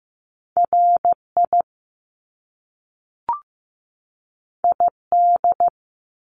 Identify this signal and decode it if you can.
Morse code, heard as a keyed tone, about 15 wpm, 700 Hz. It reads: RI ID
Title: Top Two Letter Combinations in Words 15wpm